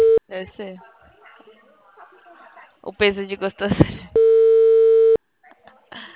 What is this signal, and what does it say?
Ringback tone (China, 450 Hz), with speech.